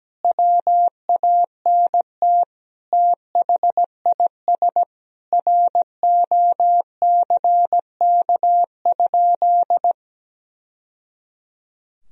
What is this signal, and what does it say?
Morse code, heard as a keyed tone, about 17 wpm, 700 Hz. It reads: WANT THIS ROCK?
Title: Sentences from Top 1000 Words - Review Part 2 - 17wpm